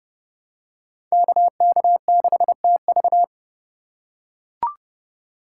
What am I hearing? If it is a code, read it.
Morse code, heard as a keyed tone, about 29 wpm, 700 Hz. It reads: KK6TV
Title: K1USN SST QSO Practice - 30wpm